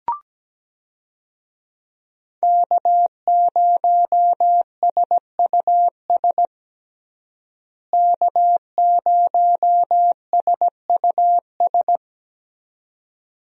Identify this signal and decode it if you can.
Morse code, heard as a keyed tone, about 17 wpm, 700 Hz. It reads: K0SUS K0SUS
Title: USA Call Signs - 5 Characters - Repeated 2x - 17wpm